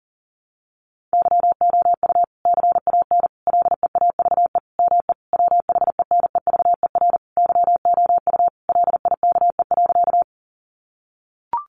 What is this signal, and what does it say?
Morse code, heard as a keyed tone, about 40 wpm, 700 Hz. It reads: YOU CAN LEAVE ME WHENEVER YOU LIKE.
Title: Sentences from Top 1200 Words 40wpm